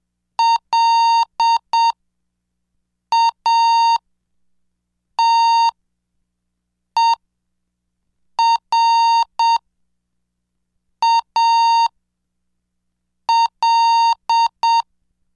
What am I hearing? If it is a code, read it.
Morse code, heard as a keyed tone, about 7 wpm, spaced as for 4 wpm, 920 Hz. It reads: LATERAL